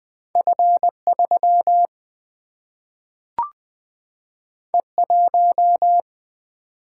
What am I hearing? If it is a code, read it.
Morse code, heard as a keyed tone, about 20 wpm, 700 Hz. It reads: F3 E1
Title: Single-Letter Single-Number 20wpm